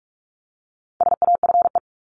Morse code, heard as a keyed tone, about 44 wpm, 700 Hz. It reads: SAFE
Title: Top 751-1000 Words 45wpm